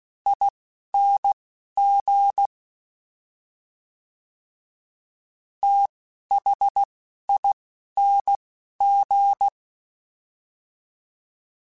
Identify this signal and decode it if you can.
Morse code, heard as a keyed tone, about 16 wpm, spaced as for 11 wpm, 800 Hz. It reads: ING THING
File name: Special INGs-mp3s / Special INGs-mp3s / Slow ing @10